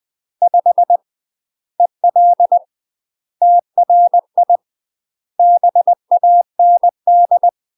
Morse code, heard as a keyed tone, about 20 wpm, 700 Hz. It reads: 5 EL TRI BAND
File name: UK MASTER QSOs TRAINING / 20WPM / UK MASTER QSOs 0049